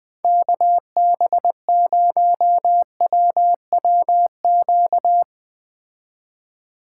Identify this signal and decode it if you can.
Morse code, heard as a keyed tone, about 20 wpm, 700 Hz. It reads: KB0WWQ